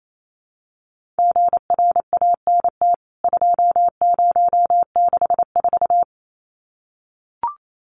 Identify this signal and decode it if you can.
Morse code, heard as a keyed tone, about 28 wpm, 700 Hz. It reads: GRANT 2064
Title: CWT Exchange 28wpm